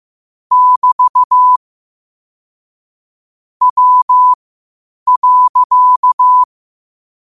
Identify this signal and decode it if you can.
Morse code, heard as a keyed tone, about 15 wpm, spaced as for 8 wpm, 1000 Hz. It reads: <BT> W.
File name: qotdfiles / 8wpm